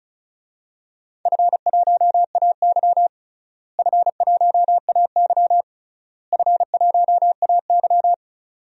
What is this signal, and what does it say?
Morse code, heard as a keyed tone, about 36 wpm, 700 Hz. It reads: F1AY F1AY F1AY
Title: DXCC-100 Call Signs - 4 Characters - Repeated 3x - 35wpm